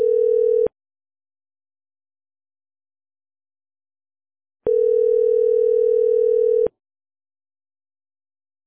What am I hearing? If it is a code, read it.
Ringback tone (North America), 440 + 480 Hz.